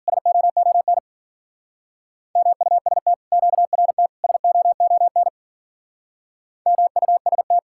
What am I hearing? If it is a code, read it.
Morse code, heard as a keyed tone, about 50 wpm, 700 Hz. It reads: SOON MUST QRT SOON MUST
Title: Speed-Racing Common QSO Phrases Encore - 40wpm